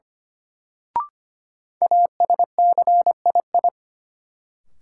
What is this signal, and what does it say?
Morse code, heard as a keyed tone, about 25 wpm, 700 Hz. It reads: ASCII
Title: Sets of 5 Words - New Word Drill - 25wpm